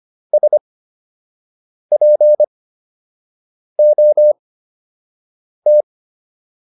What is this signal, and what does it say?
Morse code, heard as a keyed tone, about 25 wpm, spaced as for 6 wpm, 600 Hz. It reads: SPOT